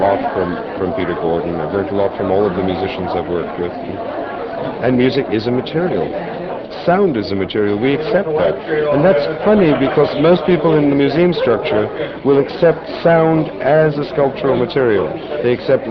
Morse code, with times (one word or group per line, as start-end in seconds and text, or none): none